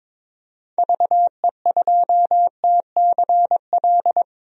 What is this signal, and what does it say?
Morse code, heard as a keyed tone, about 22 wpm, 700 Hz. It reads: VE2TCL
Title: DXCC-100 Call Signs - 6 Characters - 22wpm